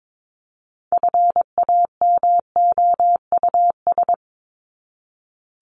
Morse code, heard as a keyed tone, about 22 wpm, 700 Hz. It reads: FAMOUS
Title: Top 1251-1500 Words 22wpm